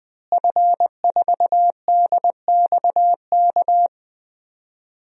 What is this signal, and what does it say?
Morse code, heard as a keyed tone, about 20 wpm, 700 Hz. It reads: F4DXK
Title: DXCC-100 Call Signs - 5 Characters - Repeated 2x - 20wpm